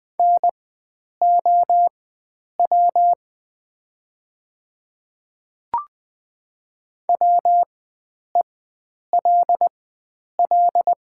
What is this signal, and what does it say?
Morse code, heard as a keyed tone, about 20 wpm, spaced as for 9 wpm, 700 Hz. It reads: NOW WELL
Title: 4x Character Spacing - Top 300 Words - 20wpm